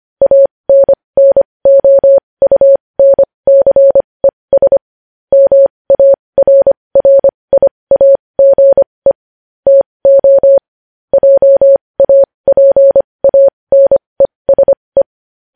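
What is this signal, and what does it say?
Morse code, heard as a keyed tone, about 25 wpm, spaced as for 20 wpm, 550 Hz. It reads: ANNOUNCES MARRIAGE TO JAPANESE
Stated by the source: Morse code news headlines sent at 20 wpm.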